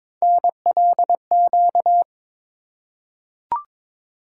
Morse code, heard as a keyed tone, about 22 wpm, 700 Hz. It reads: NLQ
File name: practice / 22wpm / Three Letter-Number Combinations - 22wpm